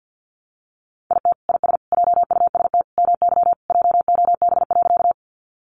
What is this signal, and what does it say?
Morse code, heard as a keyed tone, about 50 wpm, 700 Hz. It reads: IT IS JUST MY JOB.
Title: Sentences from Top 600 Words 50wpm